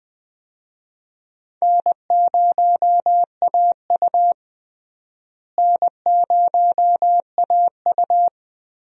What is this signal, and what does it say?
Morse code, heard as a keyed tone, about 20 wpm, 700 Hz. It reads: N0AU N0AU
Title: USA Call Signs - 4 Characters - Repeated 3x - 20wpm